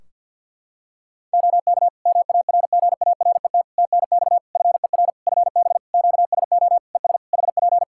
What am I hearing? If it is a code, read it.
Morse code, heard as a keyed tone, about 51 wpm, 700 Hz. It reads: OK MARGARET TNX FER FB QSO ES HP
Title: Practice QSO - Exchange 4 - Ending - 50wpm